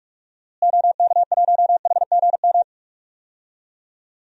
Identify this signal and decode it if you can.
Morse code, heard as a keyed tone, about 46 wpm, 700 Hz. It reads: OK1HGM
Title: DXCC-100 Call Signs - 6 Characters - Repeated 2x - 45wpm